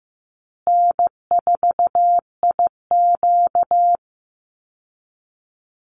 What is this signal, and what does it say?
Morse code, heard as a keyed tone, about 15 wpm, 700 Hz. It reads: N4IQ
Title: Call Signs Repeated 3x - 15wpm